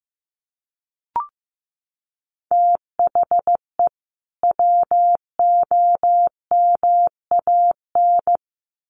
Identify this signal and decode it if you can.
Morse code, heard as a keyed tone, about 15 wpm, 700 Hz. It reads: THE WOMAN
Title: Sentences from Top 1500 Words 15wpm